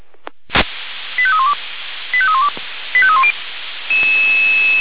Five-tone selective call (ZVEI), heard as quick runs of five tones, tones of about 70 ms. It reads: 85312 85312 85310
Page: /496/SDR/ZVEI/85310/